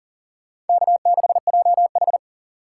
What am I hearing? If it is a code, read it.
Morse code, heard as a keyed tone, about 51 wpm, 700 Hz. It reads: K6JH